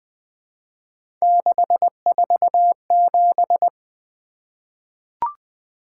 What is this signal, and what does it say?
Morse code, heard as a keyed tone, about 20 wpm, 700 Hz. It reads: 647
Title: Three Letter-Number Combinations - 20wpm